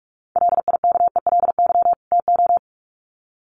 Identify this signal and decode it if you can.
Morse code, heard as a keyed tone, about 44 wpm, 700 Hz. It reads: LIKELY TO
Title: Sets of 2 Words 45wpm